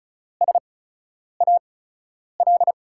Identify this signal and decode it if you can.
Morse code, heard as a keyed tone, about 36 wpm, spaced as for 10 wpm, 700 Hz. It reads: SAL
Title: USA Call Signs - 6 Characters - 8x Character Spacing - 35wpm